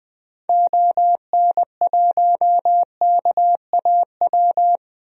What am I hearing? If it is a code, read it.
Morse code, heard as a keyed tone, about 20 wpm, 700 Hz. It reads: ON1KAW
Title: DXCC-100 Call Signs - 6 Characters - Repeated 3x - 20wpm